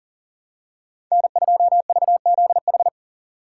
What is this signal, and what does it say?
Morse code, heard as a keyed tone, about 40 wpm, 700 Hz. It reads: N2VZH
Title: USA Call Signs - 5 Characters - 3x Character Spacing - 40wpm